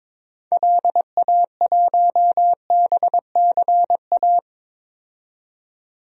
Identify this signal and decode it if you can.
Morse code, heard as a keyed tone, about 22 wpm, 700 Hz. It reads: LA1BCA
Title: DXCC-100 Call Signs - 6 Characters - 22wpm